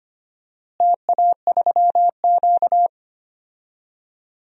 Morse code, heard as a keyed tone, about 25 wpm, 700 Hz. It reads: TA3Q
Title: DXCC-100 Call Signs - 4 Characters - Repeated 3x - 25wpm